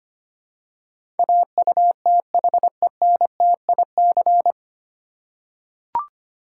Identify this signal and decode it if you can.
Morse code, heard as a keyed tone, about 25 wpm, 700 Hz. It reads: AUTHENTIC
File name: Nine Letter Words - 25wpm